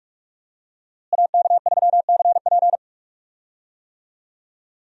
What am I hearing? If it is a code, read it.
Morse code, heard as a keyed tone, about 46 wpm, 700 Hz. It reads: AK3CP
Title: USA Call Signs - 5 Characters - Repeated 2x - 45wpm